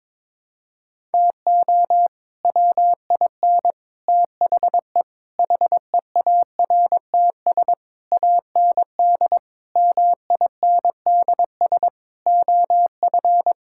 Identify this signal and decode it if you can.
Morse code, heard as a keyed tone, about 21 wpm, 700 Hz. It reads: TO WIN THE HEARTS AND MINDS OF
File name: practice / 22wpm / Sets of 8 Words - 22wpm